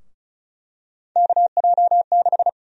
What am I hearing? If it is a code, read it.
Morse code, heard as a keyed tone, about 36 wpm, 700 Hz. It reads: KJ6
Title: USA Call Sign Prefixes - Repeated 2x - 35wpm